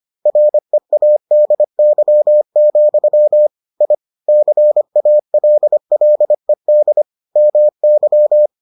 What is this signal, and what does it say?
Morse code, heard as a keyed tone, about 25 wpm, 600 Hz. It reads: READY, I CALLED MY